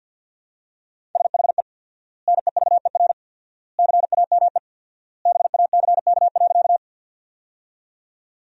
Morse code, heard as a keyed tone, about 50 wpm, 700 Hz. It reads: SHE NEVER CAME BACK.
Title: 4x Word-Spacing - Sentences from Top 300 Words - 50wpm